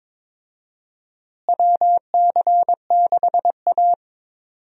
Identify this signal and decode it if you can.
Morse code, heard as a keyed tone, about 22 wpm, 700 Hz. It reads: WC6A
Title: USA Call Signs - 4 Characters - Repeated 3x - 22wpm